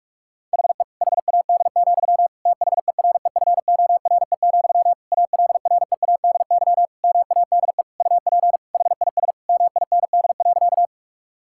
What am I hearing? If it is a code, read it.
Morse code, heard as a keyed tone, about 46 wpm, 700 Hz. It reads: HE HAD, THEREFORE, ALREADY MADE UP HIS MIND.
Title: Sentences from Top 1000 Words - Review Part 5 - 45wpm